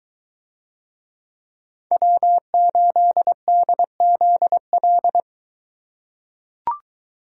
Morse code, heard as a keyed tone, about 23 wpm, 700 Hz. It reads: W8DZL